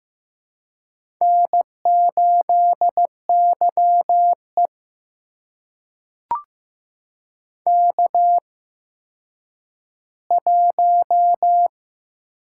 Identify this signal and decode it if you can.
Morse code, heard as a keyed tone, about 15 wpm, spaced as for 9 wpm, 700 Hz. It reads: N8YE K1
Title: USA Call Signs - 4 Characters - 8x Character Spacing - 15wpm